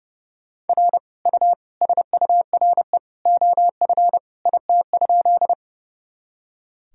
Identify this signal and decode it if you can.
Morse code, heard as a keyed tone, about 32 wpm, 700 Hz. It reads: R U SURE OF IT?
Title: Sentences from Top 300 Words and Abbreviations 30wpm